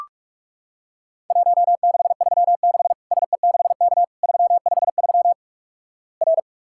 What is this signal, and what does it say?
Morse code, heard as a keyed tone, about 46 wpm, 700 Hz. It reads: 1636 SE6K 353 R
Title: Worked All Europe DX Contest - QTC Exchange - 45wpm